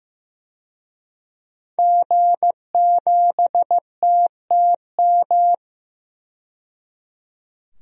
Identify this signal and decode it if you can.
Morse code, heard as a keyed tone, about 15 wpm, 700 Hz. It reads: G7TTM
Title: DXCC-100 Call Signs - 5 Characters - 15wpm